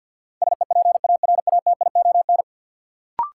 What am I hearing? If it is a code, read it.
Morse code, heard as a keyed tone, about 53 wpm, 700 Hz. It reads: SEPARATION
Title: Sets of 5 Words - New Word Drill - 50wpm